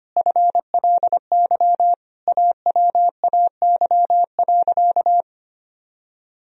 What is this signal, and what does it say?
Morse code, heard as a keyed tone, about 25 wpm, 700 Hz. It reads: FLY AWAY.
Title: Sentences from Top 1000 Words - Review Part 2 - 25wpm